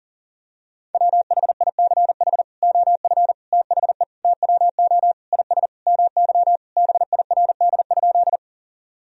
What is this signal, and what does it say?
Morse code, heard as a keyed tone, about 40 wpm, 700 Hz. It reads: WHICH OF THE TWO IS MY BIRD?
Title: Sentences from Top 1000 Words - Review Part 2 - 40wpm